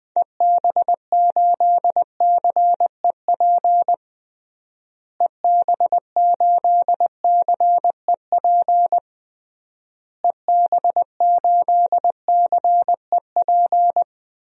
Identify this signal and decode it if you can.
Morse code, heard as a keyed tone, about 20 wpm, 700 Hz. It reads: EB8CEP EB8CEP EB8CEP
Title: DXCC-100 Call Signs - 6 Characters - Repeated 3x - 20wpm